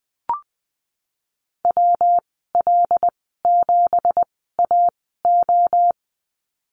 Morse code, heard as a keyed tone, about 19 wpm, spaced as for 14 wpm, 700 Hz. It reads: WL7AO
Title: 179 Lesson 29g - Slash - Call Signs 20wpm